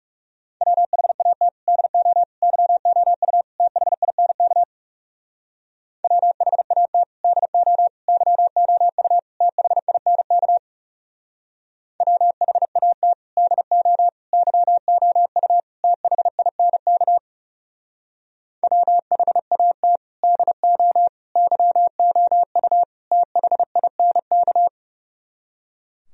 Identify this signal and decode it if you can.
Morse code, heard as a keyed tone, about 38 wpm, 700 Hz. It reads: WHAT DO YOU THINK WHAT DO YOU THINK WHAT DO YOU THINK WHAT DO YOU THINK